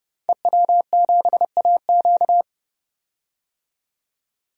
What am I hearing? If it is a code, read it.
Morse code, heard as a keyed tone, about 29 wpm, 700 Hz. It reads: EW7AQ
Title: DXCC-100 Call Signs - 5 Characters - 30wpm